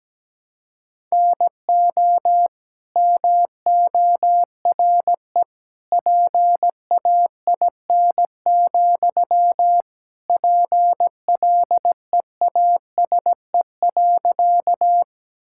Morse code, heard as a keyed tone, about 17 wpm, 700 Hz. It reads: NO MORE PAIN, PLEASE.
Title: Sentences from Top 1600 Words 17wpm